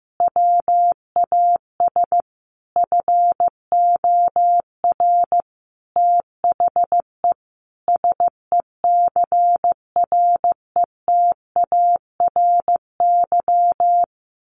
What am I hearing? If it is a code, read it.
Morse code, heard as a keyed tone, about 15 wpm, 700 Hz. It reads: WAS FOR THE SECRETARY
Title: Sentences from Top 1400 Words 15wpm